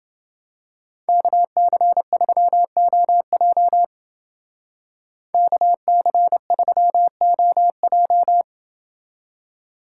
Morse code, heard as a keyed tone, about 28 wpm, 700 Hz. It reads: KC3OJ KC3OJ